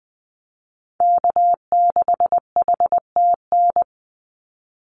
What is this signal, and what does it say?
Morse code, heard as a keyed tone, about 20 wpm, 700 Hz. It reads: K6HTN
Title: Call Signs Repeated 3x - 20wpm